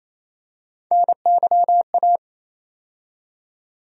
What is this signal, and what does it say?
Morse code, heard as a keyed tone, about 28 wpm, 700 Hz. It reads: NYA
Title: USA Call Sign Postfixes - 3 Letters 28wpm